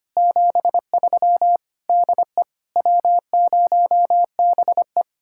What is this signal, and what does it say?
Morse code, heard as a keyed tone, about 32 wpm, 700 Hz. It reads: 73 DE W0BE